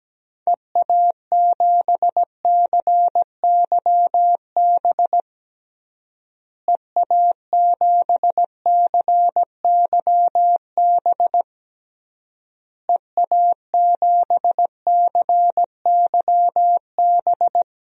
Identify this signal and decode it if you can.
Morse code, heard as a keyed tone, about 17 wpm, 700 Hz. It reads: EA7CYB EA7CYB EA7CYB
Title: DXCC-100 Call Signs - 6 Characters - Repeated 3x - 17wpm